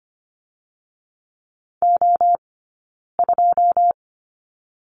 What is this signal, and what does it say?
Morse code, heard as a keyed tone, about 25 wpm, 700 Hz. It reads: O 2
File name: course / 25:10wpm / 092 Lesson 16d - Letter c - Two Characters 25wpm